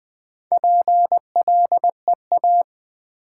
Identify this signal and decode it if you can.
Morse code, heard as a keyed tone, about 20 wpm, 700 Hz. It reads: PLEA
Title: Build A Word - 1 Letter At A Time - 5-Letter Words - 20wpm